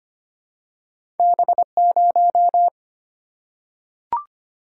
Morse code, heard as a keyed tone, about 25 wpm, 690 Hz. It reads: B0
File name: Two Letter-Number Combinations - 25wpm